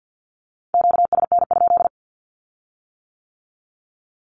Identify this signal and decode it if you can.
Morse code, heard as a keyed tone, about 50 wpm, 700 Hz. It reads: QSN?
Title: Q-Codes with Spoken Meaning - 50wpm